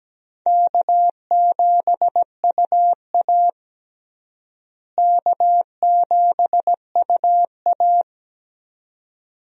Morse code, heard as a keyed tone, about 17 wpm, 700 Hz. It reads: K7UA K7UA
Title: USA Call Signs - 4 Characters - Repeated 2x - 17wpm